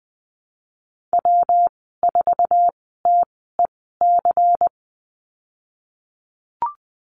Morse code, heard as a keyed tone, about 20 wpm, spaced as for 14 wpm, 700 Hz. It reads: W4TEC
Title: 128 Lesson 21g - Number 6 - Call Signs 20wpm